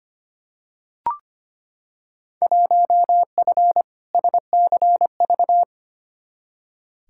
Morse code, heard as a keyed tone, about 25 wpm, 700 Hz. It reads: 1F SCV
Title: Field Day Exchanges Encore - 25wpm